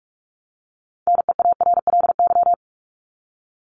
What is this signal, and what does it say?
Morse code, heard as a keyed tone, about 44 wpm, 700 Hz. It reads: NEARLY
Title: LY Suffix - Top 100 Words with Base Words 45wpm